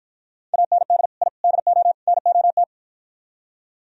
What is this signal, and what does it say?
Morse code, heard as a keyed tone, about 55 wpm, 700 Hz. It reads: AND I DO NOT